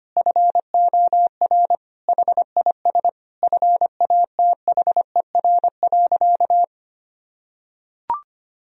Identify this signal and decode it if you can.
Morse code, heard as a keyed tone, about 25 wpm, 700 Hz. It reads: FOR HIS FATHER.
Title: Sentences from Top 1000 Words - Review Part 2 - 25wpm